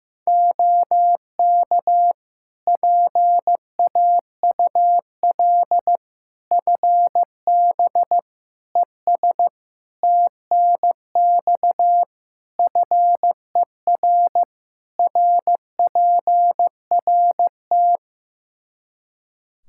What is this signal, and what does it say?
Morse code, heard as a keyed tone, about 15 wpm, 700 Hz. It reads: OK PAUL FB ES TNX FER RPRT
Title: K1USN SST QSO Practice 15wpm